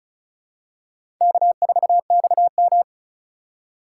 Morse code, heard as a keyed tone, about 36 wpm, 700 Hz. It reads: K4XM
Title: USA Call Signs - 4 Characters - 3x Character Spacing - 35wpm